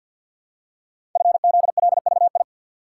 Morse code, heard as a keyed tone, about 52 wpm, 700 Hz. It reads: FZLVI
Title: Code Groups - 5 Letters - 50wpm